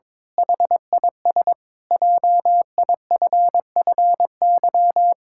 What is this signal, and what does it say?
Morse code, heard as a keyed tone, about 21 wpm, 700 Hz. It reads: HIS JIFFY
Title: Pangram Challenge v2 - 22wpm